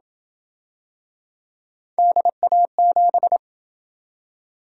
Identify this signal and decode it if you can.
Morse code, heard as a keyed tone, about 28 wpm, 700 Hz. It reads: DA7